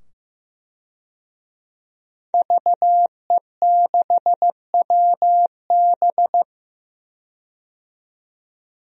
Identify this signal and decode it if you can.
Morse code, heard as a keyed tone, about 15 wpm, 700 Hz. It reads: VE6WB
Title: DXCC-100 Call Signs - 5 Characters - 15wpm